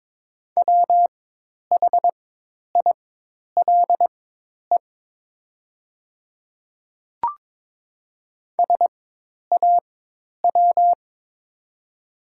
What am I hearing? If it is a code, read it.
Morse code, heard as a keyed tone, about 22 wpm, spaced as for 10 wpm, 700 Hz. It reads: WHILE SAW
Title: 4x Character Spacing - Top 300 Words - 22wpm